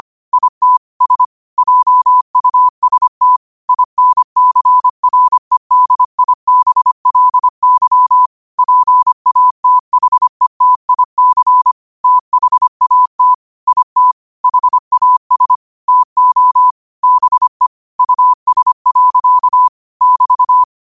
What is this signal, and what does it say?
Morse code, heard as a keyed tone, about 25 wpm, 1000 Hz. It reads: IT S JUST INCREDIBLY PATHETIC THAT IT HAS TO BE US. <BT>
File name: 25wpm